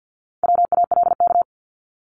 Morse code, heard as a keyed tone, about 50 wpm, 700 Hz. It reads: WALK
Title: ING Suffix - Top 100 Words with Base Words 50wpm